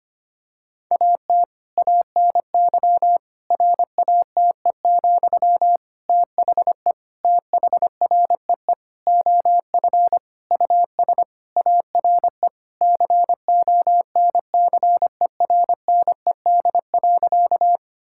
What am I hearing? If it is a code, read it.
Morse code, heard as a keyed tone, about 25 wpm, 700 Hz. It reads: AT ANY RATE, THE THREE OF US ARE CONCERNED.
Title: Sentences from Top 1000 Words - Review Part 6 - 25wpm